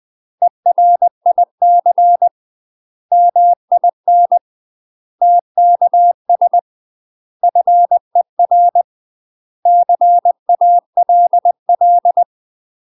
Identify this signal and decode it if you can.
Morse code, heard as a keyed tone, about 20 wpm, 700 Hz. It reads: ERIC MIN TKS FER CALL
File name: UK MASTER QSOs TRAINING/20WPM